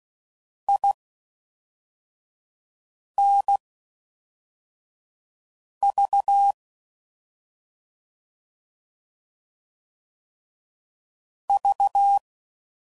Morse code, heard as a keyed tone, about 16 wpm, spaced as for 4 wpm, 800 Hz. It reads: INV V